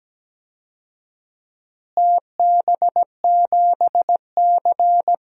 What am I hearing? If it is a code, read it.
Morse code, heard as a keyed tone, about 17 wpm, 700 Hz. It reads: TB7C
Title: DXCC-100 Call Signs - 4 Characters - Repeated 3x - 17wpm